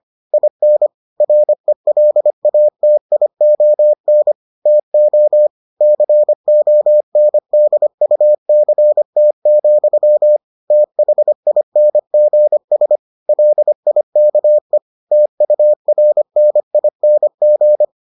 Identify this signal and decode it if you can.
Morse code, heard as a keyed tone, about 25 wpm, 600 Hz. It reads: IN RELATION TO CONDUCT, THINGS LIKE TURNING